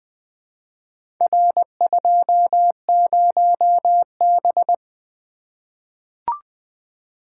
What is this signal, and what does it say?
Morse code, heard as a keyed tone, about 20 wpm, 700 Hz. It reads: R20B
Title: Four Letter-Number Combinations - 20wpm